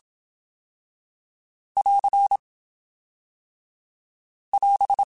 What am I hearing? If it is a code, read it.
Morse code, heard as a keyed tone, about 27 wpm, 2200 Hz. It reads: <AR> <AS>
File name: TH / TH mp3s / Fast TH @25